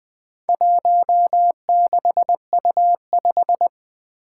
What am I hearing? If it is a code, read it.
Morse code, heard as a keyed tone, about 20 wpm, 700 Hz. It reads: 16U5
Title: Four Challenging Letter-Number Combinations 20wpm